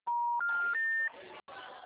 Special information tone, with noise.
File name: out- -2014- - - .16